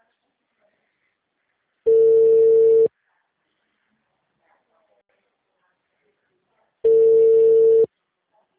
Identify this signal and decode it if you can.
Ringback tone (China, 450 Hz).